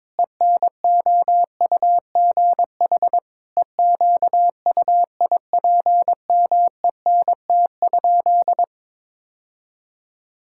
Morse code, heard as a keyed tone, about 22 wpm, 700 Hz. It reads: ENOUGH EQUIPMENT?
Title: Sentences from Top 1000 Words - Review Part 5 - 22wpm